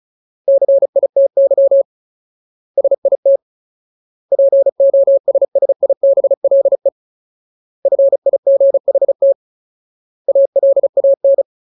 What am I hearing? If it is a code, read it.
Morse code, heard as a keyed tone, about 34 wpm, 550 Hz. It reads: CITY SIT POSSIBLE FIGHT ALAN
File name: Ultimate Word List / 35WPM Ultimate